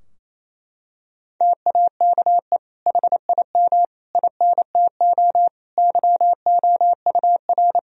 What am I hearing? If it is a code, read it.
Morse code, heard as a keyed tone, about 28 wpm, 700 Hz. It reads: TAKE HIM INTO YOUR
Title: Sentences from Top 1000 Words - Review Part 4 - 28wpm